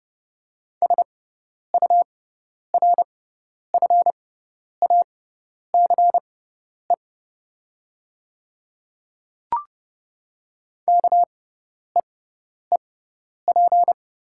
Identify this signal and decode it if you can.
Morse code, heard as a keyed tone, about 30 wpm, spaced as for 10 wpm, 700 Hz. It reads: SURFACE KEEP
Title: 6x Character Spacing - Top 500 Words - 30wpm